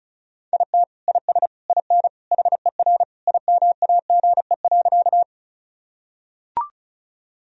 Morse code, heard as a keyed tone, about 36 wpm, 700 Hz. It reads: IT IS IN HER IMAGE.
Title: Sentences from Top 1000 Words - Review Part 4 - 35wpm